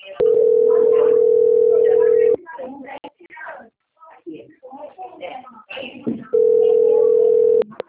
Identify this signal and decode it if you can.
Ringback tone (North America), 440 + 480 Hz, with speech.
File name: rg-1600-unknown- - - .44851